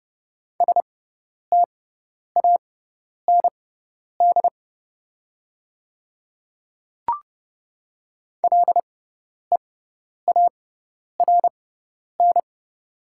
Morse code, heard as a keyed tone, about 30 wpm, spaced as for 10 wpm, 700 Hz. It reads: STAND LEARN
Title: 6x Character Spacing - Top 300 Words - 30wpm